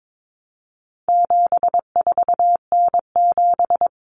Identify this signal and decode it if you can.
Morse code, heard as a keyed tone, about 22 wpm, 700 Hz. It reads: 74N7
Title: Common Cut Numbers - 4-Digit 22wpm